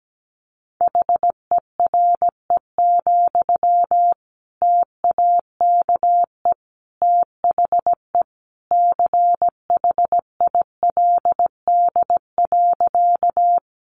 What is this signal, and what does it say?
Morse code, heard as a keyed tone, about 17 wpm, 700 Hz. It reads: HERE, TAKE THE CHILD.
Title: Sentences from Top 800 Words 17wpm